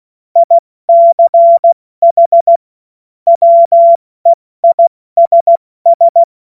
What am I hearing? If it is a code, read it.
Morse code, heard as a keyed tone, about 16 wpm, spaced as for 14 wpm, 680 Hz. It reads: ICH WEISS
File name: Morsekurs / Witze / Witz 03. Mai 2022 70 F80